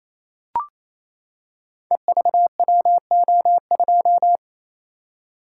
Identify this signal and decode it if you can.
Morse code, heard as a keyed tone, about 27 wpm, 700 Hz. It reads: EVWO2
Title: Code Groups - 5 Letter:Numbers - 28wpm